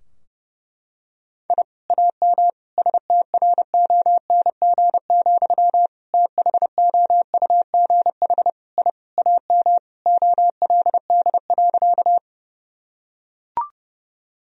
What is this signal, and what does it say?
Morse code, heard as a keyed tone, about 30 wpm, 700 Hz. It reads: I AM STRONG, THOUGH I AM OLD.
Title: Sentences from Top 1000 Words - Review Part 3 - 30wpm